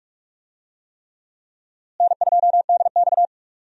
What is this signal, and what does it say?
Morse code, heard as a keyed tone, about 46 wpm, 700 Hz. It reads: N2DX
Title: USA Call Signs - 4 Characters - Repeated 3x - 45wpm